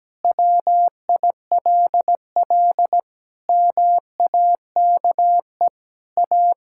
Morse code, heard as a keyed tone, about 17 wpm, 700 Hz. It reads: WILL MAKE A
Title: Pangram Challenge v2 - 17wpm